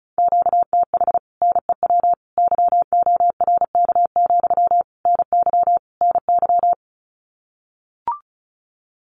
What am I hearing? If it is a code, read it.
Morse code, heard as a keyed tone, about 34 wpm, 700 Hz. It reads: QTH NEW YORK, NY NY
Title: Common QSO Phrases Encore 35wpm